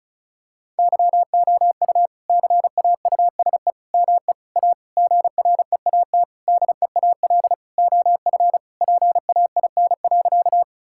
Morse code, heard as a keyed tone, about 36 wpm, 700 Hz. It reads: YOU CAUSE ME A GREAT DEAL OF PAIN.
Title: Sentences from Top 1600 Words 35wpm